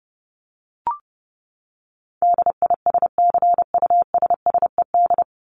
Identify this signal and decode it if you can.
Morse code, heard as a keyed tone, about 30 wpm, 700 Hz. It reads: DISCUSSED
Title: ED Suffix - Top 100 Words with Base Words 30wpm